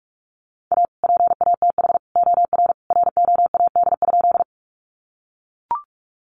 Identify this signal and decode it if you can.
Morse code, heard as a keyed tone, about 44 wpm, 700 Hz. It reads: A PATH OR ROAD?
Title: Sentences from Top 1400 Words 45wpm